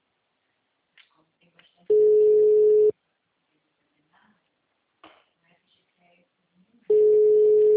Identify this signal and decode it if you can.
Ringback tone (Europe, 425 Hz), with speech.